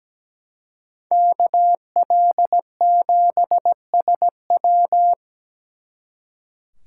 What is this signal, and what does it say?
Morse code, heard as a keyed tone, about 20 wpm, 700 Hz. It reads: KL7SW
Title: Speed Racing - USA 2x2 Call Signs 17wpm